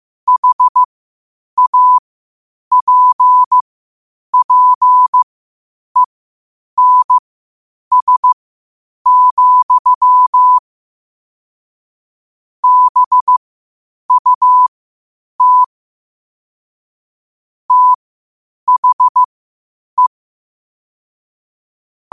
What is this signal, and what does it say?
Morse code, heard as a keyed tone, about 15 wpm, spaced as for 8 wpm, 1000 Hz. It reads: HAPPENS, BUT THE
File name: qotdfiles / 8wpm